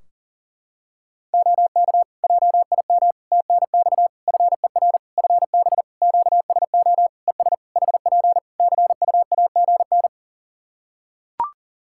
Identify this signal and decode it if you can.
Morse code, heard as a keyed tone, about 40 wpm, 700 Hz. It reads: OK JIM TNX FER FB QSO ES HP CUAGN
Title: Practice QSO - Exchange 4 - Ending - 40wpm